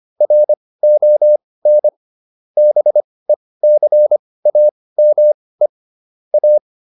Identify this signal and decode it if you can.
Morse code, heard as a keyed tone, about 25 wpm, spaced as for 18 wpm, 600 Hz. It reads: RON BECAME A